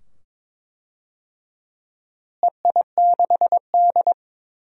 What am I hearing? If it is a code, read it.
Morse code, heard as a keyed tone, about 22 wpm, 700 Hz. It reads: EI6D
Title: DXCC-100 Call Signs - 4 Characters - Repeated 2x - 22wpm